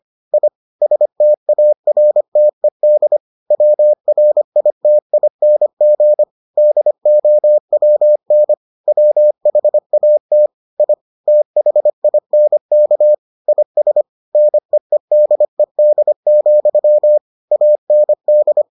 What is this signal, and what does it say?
Morse code, heard as a keyed tone, about 25 wpm, 600 Hz. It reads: I STARTED WRITING DOWN WHAT I THINK IS NEEDED, AND